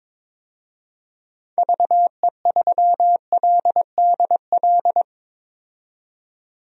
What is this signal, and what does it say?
Morse code, heard as a keyed tone, about 22 wpm, 700 Hz. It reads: VE3LDL
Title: DXCC-100 Call Signs - 6 Characters - 22wpm